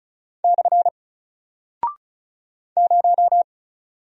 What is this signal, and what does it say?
Morse code, heard as a keyed tone, about 36 wpm, 700 Hz. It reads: / 0
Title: Single Character (Letter:Number:Punctuation:Prosign) Alt 35wpm